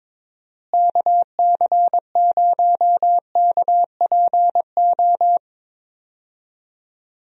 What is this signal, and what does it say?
Morse code, heard as a keyed tone, about 21 wpm, 700 Hz. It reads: KC0KPO
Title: USA Call Signs - 6 Characters - Repeated 3x - 22wpm